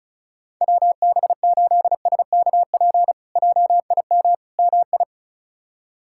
Morse code, heard as a keyed tone, about 36 wpm, 700 Hz. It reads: WB8SKP JIM MI
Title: K1USN SST QSO Practice - 35wpm